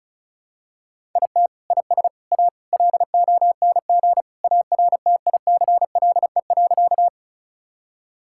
Morse code, heard as a keyed tone, about 34 wpm, 700 Hz. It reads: IT IS A LONG ARTICLE.
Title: Sentences from Top 1000 Words - Review Part 5 - 35wpm